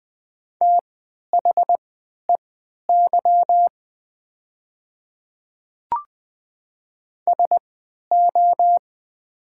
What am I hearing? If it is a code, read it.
Morse code, heard as a keyed tone, about 20 wpm, spaced as for 11 wpm, 700 Hz. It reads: THEY SO
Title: 3x Character Spacing - Top 100 Words - 20wpm